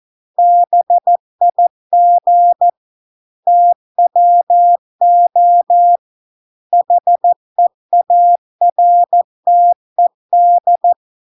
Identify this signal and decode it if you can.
Morse code, heard as a keyed tone, about 14 wpm, 700 Hz. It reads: BIG TWO HEARTED